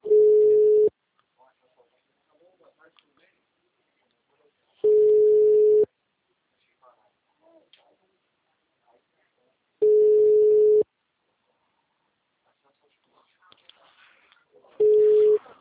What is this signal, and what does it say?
Ringback tone (Europe, 425 Hz), with speech.